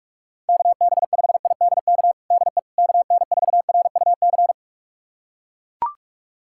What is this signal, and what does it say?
Morse code, heard as a keyed tone, about 42 wpm, 700 Hz. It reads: KB5IDK DE KN4RUC
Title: USA Callsign DE Callsign - 6 Characters - 45wpm